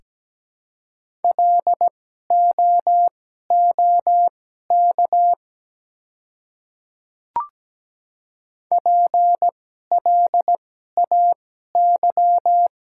Morse code, heard as a keyed tone, about 17 wpm, spaced as for 12 wpm, 700 Hz. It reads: LOOK PLAY
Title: 2x Character Spacing - Top 300 Words - 17wpm